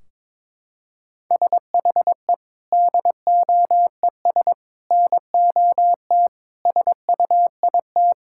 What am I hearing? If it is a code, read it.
Morse code, heard as a keyed tone, about 22 wpm, 700 Hz. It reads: SHE DOES NOT SUIT
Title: Sentences from Top 1600 Words 22wpm